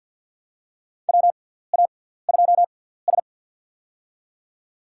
Morse code, heard as a keyed tone, about 50 wpm, spaced as for 17 wpm, 700 Hz. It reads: WA2S
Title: USA Call Signs - 4 Characters - 6x Character Spacing - 50wpm